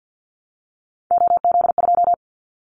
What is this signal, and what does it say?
Morse code, heard as a keyed tone, about 50 wpm, 700 Hz. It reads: OZ2
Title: Callsign Prefixes for Top 50 DXCC - 50wpm